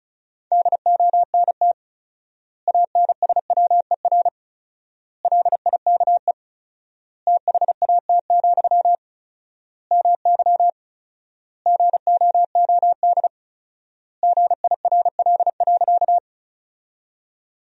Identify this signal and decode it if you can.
Morse code, heard as a keyed tone, about 34 wpm, 700 Hz. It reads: DONT ANSWER LIKE THAT, MY GOOD GIRL.
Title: 4x Word-Spacing - Sentences from Top 400 Words - 35wpm